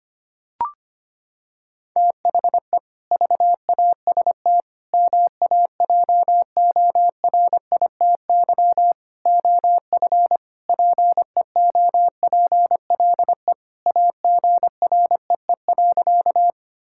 Morse code, heard as a keyed tone, about 25 wpm, 700 Hz. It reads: THE VAST MAJORITY OF PEOPLE AGREE.
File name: practice / 25wpm / Sentences with 4-Word Phrases - 25wpm